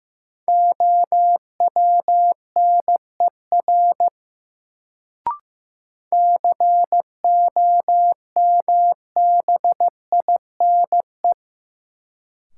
Morse code, heard as a keyed tone, about 15 wpm, 700 Hz. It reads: OWNER COMBINE
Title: Sentences from Top 1600 Words - New Word Drill - 15wpm